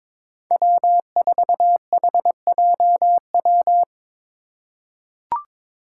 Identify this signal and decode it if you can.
Morse code, heard as a keyed tone, about 22 wpm, 700 Hz. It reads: W4HJW
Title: Five Challenging Letter-Number Combinations 22wpm